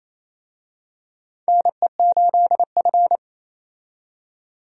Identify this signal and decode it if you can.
Morse code, heard as a keyed tone, about 27 wpm, 700 Hz. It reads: NE8F
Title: USA Call Signs - 4 Characters - 28wpm